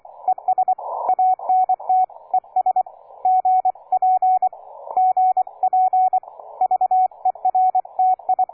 Morse code, heard as a keyed tone, about 24 wpm, 740 Hz, with radio noise. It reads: ES ANT ES GP GP VERTI
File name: cw forum / K9DX 1818 le 17-12-2008 05h47